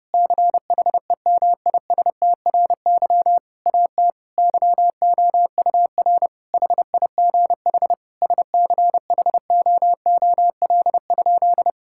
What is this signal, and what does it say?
Morse code, heard as a keyed tone, about 30 wpm, 700 Hz. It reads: CHEMISTRY AT YOUR HIGH SCHOOL?
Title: Sentences from Top 1000 Words - Review Part 6 - 30wpm